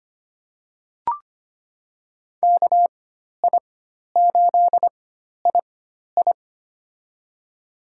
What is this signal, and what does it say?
Morse code, heard as a keyed tone, about 25 wpm, spaced as for 12 wpm, 700 Hz. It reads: KI8II
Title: USA Call Signs - 4x Character Spacing - 25wpm